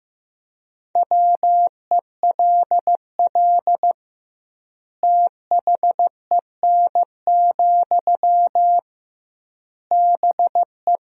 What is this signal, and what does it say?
Morse code, heard as a keyed tone, about 15 wpm, 700 Hz. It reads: WELL THEN, BE
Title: 2x Word-Spacing - Sentences from Top 500 Words - 15wpm